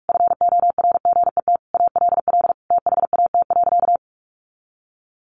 Morse code, heard as a keyed tone, about 44 wpm, 700 Hz. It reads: FORGET ALL THAT.
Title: Sentences from Top 1200 Words 45wpm